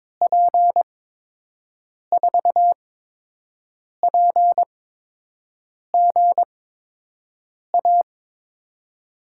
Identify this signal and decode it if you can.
Morse code, heard as a keyed tone, about 22 wpm, spaced as for 6 wpm, 700 Hz. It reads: P4PGA
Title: USA Call Signs - 8x Character Spacing - 22wpm